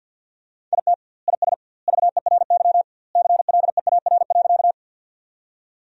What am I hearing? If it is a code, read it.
Morse code, heard as a keyed tone, about 50 wpm, 700 Hz. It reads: IT IS VERY CLEAR.
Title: 2x Word-Spacing - Sentences from Top 500 Words - 50wpm